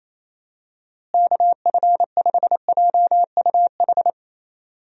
Morse code, heard as a keyed tone, about 28 wpm, 700 Hz. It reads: KF5JUH
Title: USA Call Signs - 6 Characters - Repeated 2x - 28wpm